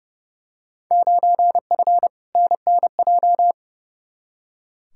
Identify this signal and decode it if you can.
Morse code, heard as a keyed tone, about 30 wpm, 700 Hz. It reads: 9F NNJ
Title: Field Day Exchanges Encore - 30wpm